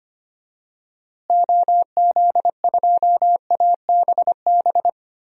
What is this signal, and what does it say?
Morse code, heard as a keyed tone, about 25 wpm, 700 Hz. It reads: OZ2ABB
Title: DXCC-100 Call Signs - 6 Characters - Repeated 3x - 25wpm